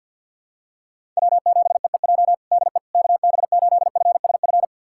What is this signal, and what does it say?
Morse code, heard as a keyed tone, about 50 wpm, 700 Hz. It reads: W7EEJ DE KB8FSF
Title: USA Callsign DE Callsign 50wpm